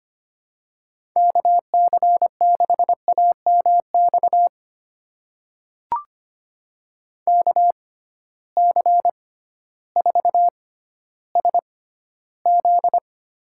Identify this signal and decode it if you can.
Morse code, heard as a keyed tone, about 25 wpm, spaced as for 13 wpm, 700 Hz. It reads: KC6AMX KC4SZ
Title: USA Call Signs - 6 Characters - 6x Character Spacing - 25wpm